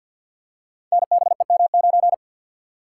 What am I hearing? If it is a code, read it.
Morse code, heard as a keyed tone, about 53 wpm, 700 Hz. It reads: NBEM9
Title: Code Groups - 5 Letter:Numbers - 50wpm